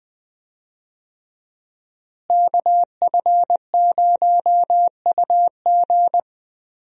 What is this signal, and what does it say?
Morse code, heard as a keyed tone, about 20 wpm, 700 Hz. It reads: KF0UG
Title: USA Call Signs - 5 Characters - Repeated 3x - 20wpm